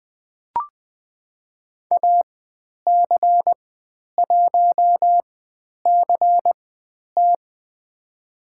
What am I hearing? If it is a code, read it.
Morse code, heard as a keyed tone, about 20 wpm, spaced as for 10 wpm, 700 Hz. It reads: AC1CT
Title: 095 Lesson 16g - Letter c - Call Signs 20wpm